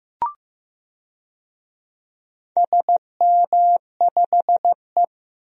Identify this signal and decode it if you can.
Morse code, heard as a keyed tone, about 15 wpm, 700 Hz. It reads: SM5E
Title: DXCC-100 Call Signs - 4 Characters - Repeated 2x - 15wpm